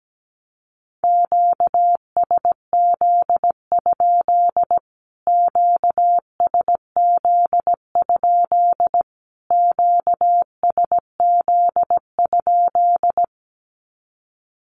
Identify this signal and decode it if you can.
Morse code, heard as a keyed tone, about 17 wpm, 700 Hz. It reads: QSZ? QSZ? QSZ?
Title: Q-Codes Repeated 3x 17wpm